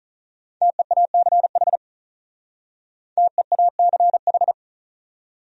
Morse code, heard as a keyed tone, about 43 wpm, 700 Hz. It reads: TEACH TEACH
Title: Speed-Racing Top 500 Words 35wpm